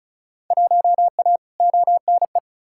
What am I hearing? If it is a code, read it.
Morse code, heard as a keyed tone, about 34 wpm, 700 Hz. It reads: 1A ONE
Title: Field Day Exchanges - 35wpm